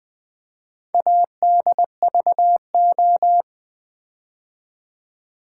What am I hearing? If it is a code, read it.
Morse code, heard as a keyed tone, about 20 wpm, 700 Hz. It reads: ADVO
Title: Build A Word - 2 Letters At A Time - 8-Letter Words - 20wpm